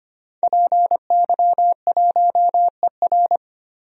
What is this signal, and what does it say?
Morse code, heard as a keyed tone, about 25 wpm, 700 Hz. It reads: PY1ER
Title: DXCC-100 Call Signs - 5 Characters - 25wpm